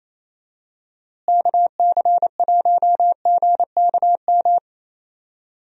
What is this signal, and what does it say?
Morse code, heard as a keyed tone, about 28 wpm, 700 Hz. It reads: KC1GKM
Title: USA Call Signs - 6 Characters - Repeated 2x - 28wpm